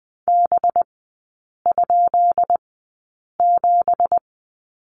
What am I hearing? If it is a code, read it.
Morse code, heard as a keyed tone, about 20 wpm, spaced as for 8 wpm, 700 Hz. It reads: B?7
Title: 184 Lesson 30e - Letter b - Three Characters 20wpm